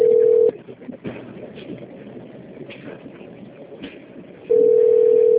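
Ringback tone (North America), 440 + 480 Hz.
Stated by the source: Outbound telephone call placed from Piracicaba - SP.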